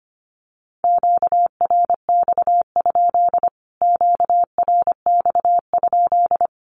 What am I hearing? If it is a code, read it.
Morse code, heard as a keyed tone, about 25 wpm, 700 Hz. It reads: QRX? QRX?
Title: Q-Codes Repeated 3x 25wpm